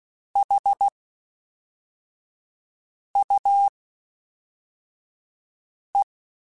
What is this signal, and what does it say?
Morse code, heard as a keyed tone, about 16 wpm, spaced as for 4 wpm, 800 Hz. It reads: HUE
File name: Misc / Misc-mp3s / WordPairs sp @5